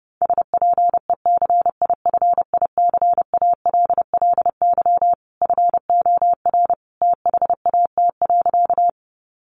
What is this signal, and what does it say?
Morse code, heard as a keyed tone, about 30 wpm, 700 Hz. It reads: SPECIFICALLY FOR THAT.
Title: Sentences from Top 1100 Words 30wpm